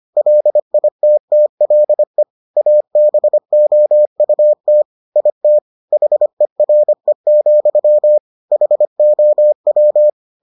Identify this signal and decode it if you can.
Morse code, heard as a keyed tone, about 25 wpm, 600 Hz. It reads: LITTLE ABOUT IT HERE, HOW